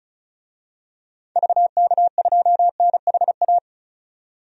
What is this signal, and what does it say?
Morse code, heard as a keyed tone, about 36 wpm, 700 Hz. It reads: VK2NHA
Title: DXCC-100 Call Signs - 6 Characters - 35wpm